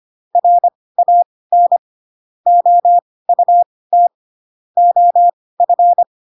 Morse code, heard as a keyed tone, about 25 wpm, spaced as for 18 wpm, 700 Hz. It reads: RAN OUT OF